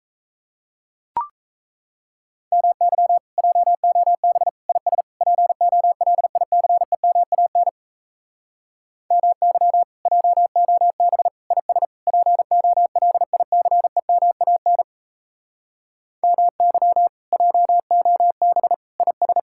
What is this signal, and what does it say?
Morse code, heard as a keyed tone, about 38 wpm, 700 Hz. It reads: MY JOB IS POLICEMAN MY JOB IS POLICEMAN MY JOB IS